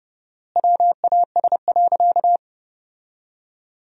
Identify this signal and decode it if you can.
Morse code, heard as a keyed tone, about 32 wpm, 700 Hz. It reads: WAS.
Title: 4x Word-Spacing - Sentences from Top 300 Words - 30wpm